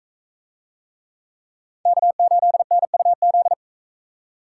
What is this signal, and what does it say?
Morse code, heard as a keyed tone, about 44 wpm, 700 Hz. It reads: K8NUZ